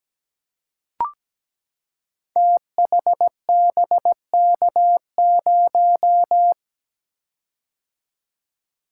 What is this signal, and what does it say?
Morse code, heard as a keyed tone, about 17 wpm, 700 Hz. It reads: THBK0
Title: Code Groups - 5 Letter:Numbers - 17wpm